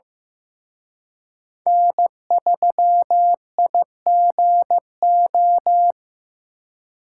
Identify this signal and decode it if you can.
Morse code, heard as a keyed tone, about 17 wpm, 700 Hz. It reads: N3IGO
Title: Speed Racing - USA 1x3 Call Signs 15wpm